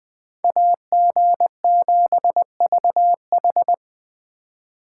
Morse code, heard as a keyed tone, about 21 wpm, 700 Hz. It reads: AG7VH